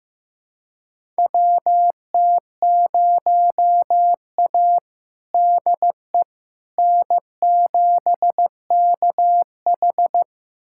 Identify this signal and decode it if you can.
Morse code, heard as a keyed tone, about 15 wpm, 700 Hz. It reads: WT0A DE N7KH
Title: USA Callsign DE Callsign - 4 Characters - 15wpm